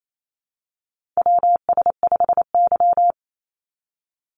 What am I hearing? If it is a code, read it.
Morse code, heard as a keyed tone, about 28 wpm, 700 Hz. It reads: WS5Y
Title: USA Call Signs 28wpm